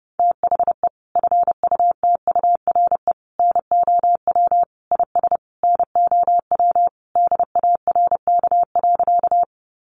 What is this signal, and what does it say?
Morse code, heard as a keyed tone, about 30 wpm, 700 Hz. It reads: THE FUTURE NOW IS NOW DARK.
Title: Sentences from Top 700 Words 30wpm